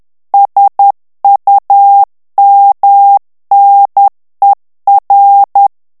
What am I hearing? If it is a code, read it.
Morse code, heard as a keyed tone, about 10 wpm, 800 Hz. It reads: SUMNER
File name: ANGLAIS texte clair 3 vitesse 10 mots minute